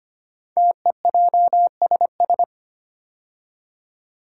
Morse code, heard as a keyed tone, about 25 wpm, 700 Hz. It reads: TEJSS
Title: Code Groups - 5 Letters - 25wpm